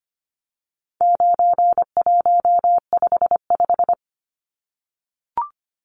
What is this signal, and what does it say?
Morse code, heard as a keyed tone, about 25 wpm, 700 Hz. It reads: 9155
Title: Numbers - 4-Digit 25wpm